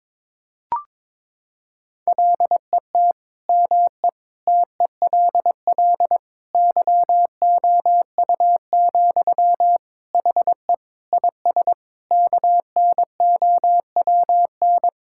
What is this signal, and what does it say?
Morse code, heard as a keyed tone, about 22 wpm, 700 Hz. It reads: LET ME TELL YOU, HE IS KNOWN
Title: Sentences from Top 1600 Words 22wpm